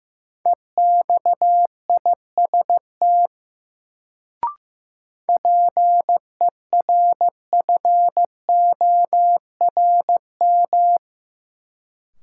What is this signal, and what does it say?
Morse code, heard as a keyed tone, about 16 wpm, 700 Hz. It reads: EXIST PERFORM
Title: Sentences from Top 1000 Words - Review Part 5 - New Word Drill - 15wpm